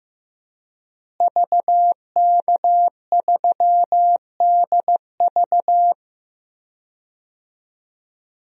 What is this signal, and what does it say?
Morse code, heard as a keyed tone, about 15 wpm, 700 Hz. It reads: VK3DV
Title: DXCC-100 Call Signs - 5 Characters - Repeated 2x - 15wpm